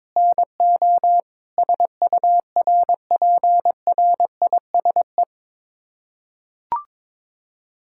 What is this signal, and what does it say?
Morse code, heard as a keyed tone, about 22 wpm, 700 Hz. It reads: NO SURPRISE
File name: practice / 22wpm / Sets of 6 Words - 22wpm